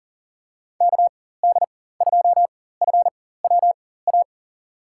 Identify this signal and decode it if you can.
Morse code, heard as a keyed tone, about 40 wpm, spaced as for 19 wpm, 700 Hz. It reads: KD2FWA
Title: USA Call Signs - 4x Character Spacing - 40wpm